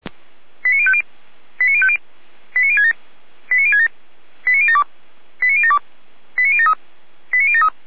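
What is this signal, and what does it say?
Five-tone selective call (ZVEI), heard as quick runs of five tones, tones of about 70 ms. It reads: 89050 89050 89067 89067 89072 89072 89073 89073
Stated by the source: ZVEI five-tone selective call 89014, received via SDR.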